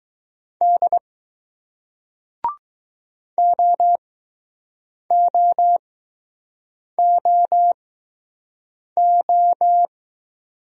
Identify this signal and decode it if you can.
Morse code, heard as a keyed tone, about 20 wpm, spaced as for 6 wpm, 700 Hz. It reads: D OOOO